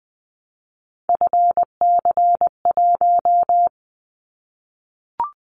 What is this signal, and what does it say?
Morse code, heard as a keyed tone, about 20 wpm, 700 Hz. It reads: FC1
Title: Callsign Prefixes for Top 50 DXCC - 20wpm